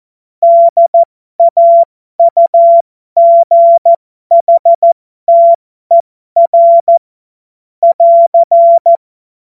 Morse code, heard as a keyed tone, about 14 wpm, spaced as for 12 wpm, 680 Hz. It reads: DAUGHTER <AR>
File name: Morsekurs / Witze